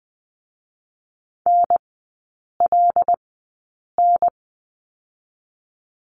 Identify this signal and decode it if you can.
Morse code, heard as a keyed tone, about 20 wpm, spaced as for 8 wpm, 690 Hz. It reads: NLN
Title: 087 Lesson 15e - Number 5 - Three Characters 20wpm